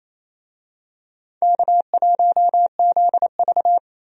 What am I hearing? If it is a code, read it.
Morse code, heard as a keyed tone, about 28 wpm, 700 Hz. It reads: K1ZV
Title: USA Call Signs - 4 Characters - Repeated 2x - 28wpm